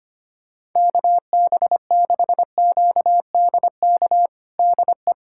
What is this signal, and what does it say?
Morse code, heard as a keyed tone, about 25 wpm, 700 Hz. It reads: KB6QDK DE
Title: USA Callsign DE Callsign 25wpm